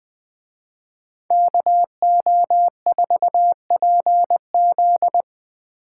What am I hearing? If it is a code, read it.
Morse code, heard as a keyed tone, about 20 wpm, 700 Hz. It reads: KO4PZ